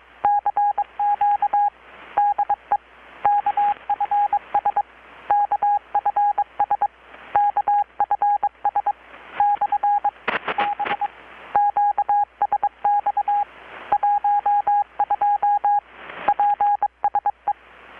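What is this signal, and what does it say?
Morse code, heard as a keyed tone, about 21 wpm, 820 Hz, with radio noise. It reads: CQ DE KFS KFS KFS/B QSX 12 PSE
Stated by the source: Morse from KPH's Night of Nights XX maritime radiotelegraphy event, received in Chiapas, Mexico about 3,800 km away on a Kenwood R-600 with a grounded 10 m long-wire antenna.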